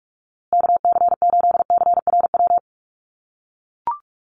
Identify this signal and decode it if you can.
Morse code, heard as a keyed tone, about 44 wpm, 700 Hz. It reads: KC8CRW
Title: USA Call Signs Repeated 2x 45wpm